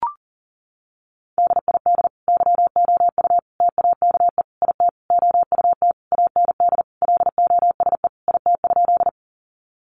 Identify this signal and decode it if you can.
Morse code, heard as a keyed tone, about 40 wpm, 700 Hz. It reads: DID YOU TAKE IT OUT AND LOSE IT?
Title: Sentences from Top 1100 Words 40wpm